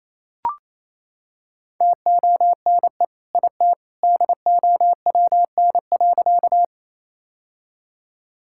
Morse code, heard as a keyed tone, about 28 wpm, 700 Hz. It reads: TONE IT DOWN.
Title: Sentences from Top 1600 Words 28wpm